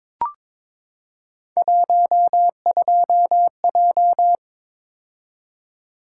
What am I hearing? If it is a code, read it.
Morse code, heard as a keyed tone, about 22 wpm, 700 Hz. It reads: 12J
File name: practice / 22wpm / Three Letter-Number Combinations - 22wpm